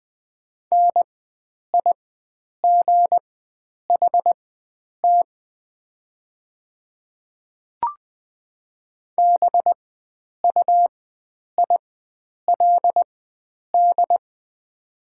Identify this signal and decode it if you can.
Morse code, heard as a keyed tone, about 20 wpm, spaced as for 9 wpm, 700 Hz. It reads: NIGHT BUILD
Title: 4x Character Spacing - Top 500 Words - 20wpm